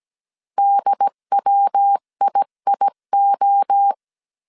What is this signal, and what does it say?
Morse code, heard as a keyed tone, about 17 wpm, 800 Hz. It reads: DWIIO
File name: CW / DET / 1 3 1500C S A 16wpm 2141